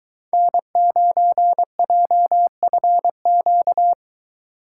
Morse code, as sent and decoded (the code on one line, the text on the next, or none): -. ----. .--- ..-. --.-
N9JFQ